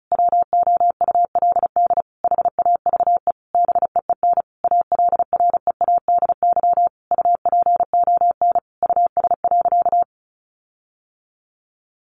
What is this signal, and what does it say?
Morse code, heard as a keyed tone, about 34 wpm, 700 Hz. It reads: WOULD HAVE BEEN ALREADY UPON US.
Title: Sentences from Top 1500 Words 35wpm